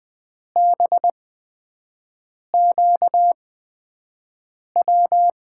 Morse code, heard as a keyed tone, about 20 wpm, spaced as for 5 wpm, 700 Hz. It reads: BQW
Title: USA Call Signs - 6 Characters - 8x Character Spacing - 20wpm